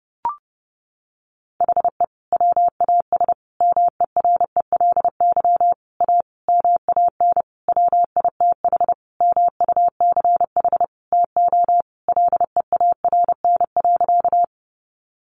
Morse code, heard as a keyed tone, about 30 wpm, 700 Hz. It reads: HE WAS MERELY A MAN WITH MUCH TO LEARN.
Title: Sentences from Top 1400 Words 30wpm